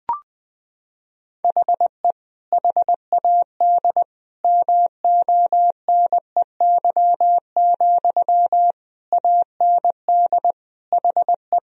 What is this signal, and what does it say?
Morse code, heard as a keyed tone, about 20 wpm, 700 Hz. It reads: HE HAD MONEY, AND HE
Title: Sentences from Top 1000 Words - Review Part 3 - 20wpm